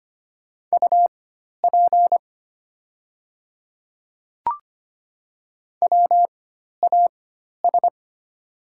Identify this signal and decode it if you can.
Morse code, heard as a keyed tone, about 25 wpm, spaced as for 12 wpm, 700 Hz. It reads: UP WAS
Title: 4x Character Spacing - Top 100 Words - 25wpm